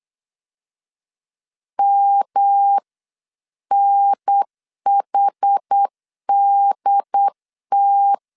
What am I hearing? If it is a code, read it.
Morse code, heard as a keyed tone, about 8 wpm, 800 Hz. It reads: MNHDT